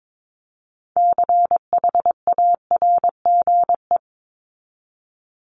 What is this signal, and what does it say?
Morse code, heard as a keyed tone, about 22 wpm, 700 Hz. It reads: CHARGE
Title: Top 751-1000 Words 22wpm